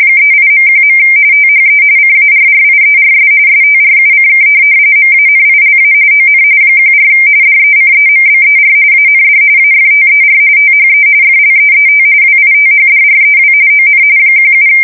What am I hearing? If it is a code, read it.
RTTY, about 45.45 baud. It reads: 38.1. THE CARRIERS STRUCK MANILA ON 14 DECEMBER AND AFTER THE FLEET WITHDREW ON 17
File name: fsk170 / USS-Massachusetts